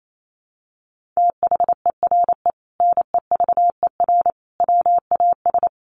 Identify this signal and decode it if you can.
Morse code, heard as a keyed tone, about 28 wpm, 700 Hz. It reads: THERE NEVER WAS
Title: Sentences from Top 1500 Words 28wpm